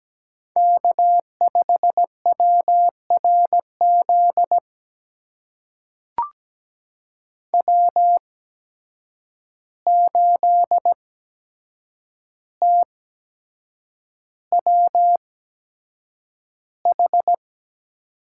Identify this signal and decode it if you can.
Morse code, heard as a keyed tone, about 17 wpm, spaced as for 7 wpm, 700 Hz. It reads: K5WRZ W8TWH
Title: USA Call Signs - 5 Characters - 8x Character Spacing - 17wpm